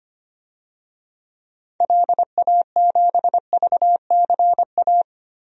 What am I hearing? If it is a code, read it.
Morse code, heard as a keyed tone, about 25 wpm, 700 Hz. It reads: LA7VCA
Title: DXCC-100 Call Signs - 6 Characters - 25wpm